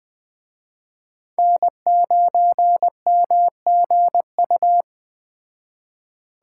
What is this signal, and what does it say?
Morse code, heard as a keyed tone, about 21 wpm, 700 Hz. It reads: N9MGU